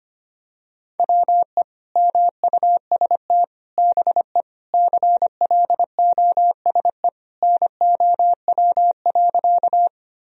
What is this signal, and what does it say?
Morse code, heard as a keyed tone, about 25 wpm, 700 Hz. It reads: WE MUST BE CLOSE NOW.
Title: Sentences from Top 1000 Words - Review Part 1 - 25wpm